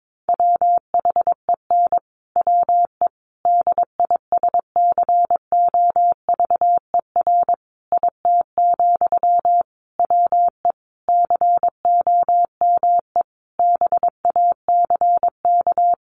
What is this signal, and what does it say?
Morse code, heard as a keyed tone, about 22 wpm, 700 Hz. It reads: WHEN WE DISCOVER IT, WE COME BACK
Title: Sentences from Top 1100 Words 22wpm